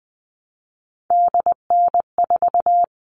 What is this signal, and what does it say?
Morse code, heard as a keyed tone, about 20 wpm, 700 Hz. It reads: DN4
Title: All Cut Numbers - 3-Digit 20wpm